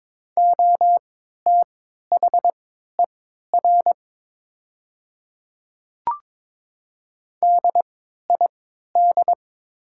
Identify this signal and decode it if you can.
Morse code, heard as a keyed tone, about 23 wpm, spaced as for 13 wpm, 700 Hz. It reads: OTHER DID
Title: 3x Character Spacing - Top 200 Words - 22wpm